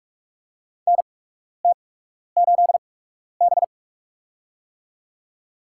Morse code, heard as a keyed tone, about 44 wpm, spaced as for 12 wpm, 690 Hz. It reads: NT8B
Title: USA Call Signs - 4 Characters - 8x Character Spacing - 45wpm